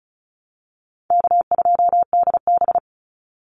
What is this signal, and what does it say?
Morse code, heard as a keyed tone, about 34 wpm, 700 Hz. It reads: K2DB
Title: Easy Call Signs Repeated 3x - 35wpm